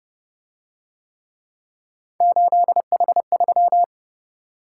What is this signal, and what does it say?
Morse code, heard as a keyed tone, about 27 wpm, 700 Hz. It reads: 8H3